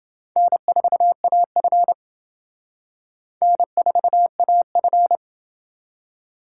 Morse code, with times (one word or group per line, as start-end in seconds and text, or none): 0.4-1.9 N4AF
3.4-5.2 N4AF